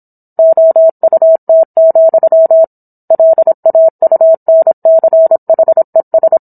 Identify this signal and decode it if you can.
Morse code, heard as a keyed tone, about 27 wpm, 650 Hz. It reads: OUT, LAUNCHES